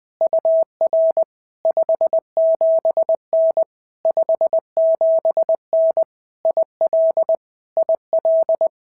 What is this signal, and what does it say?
Morse code, heard as a keyed tone, about 20 wpm, 650 Hz. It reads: UR 57N 57N IL IL
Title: POTA QSO Practice - 20wpm